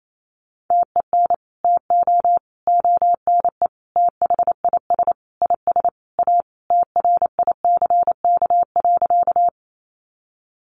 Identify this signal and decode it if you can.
Morse code, heard as a keyed tone, about 28 wpm, 700 Hz. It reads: TEN TO ONE THIS IS A TRICK.
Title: Sentences from Top 1500 Words 28wpm